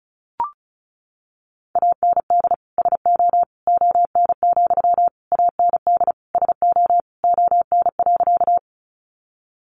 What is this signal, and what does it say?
Morse code, heard as a keyed tone, about 34 wpm, 700 Hz. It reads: AND SO ON, AND SO ON.
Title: Sentences from Top 100 Words 35wpm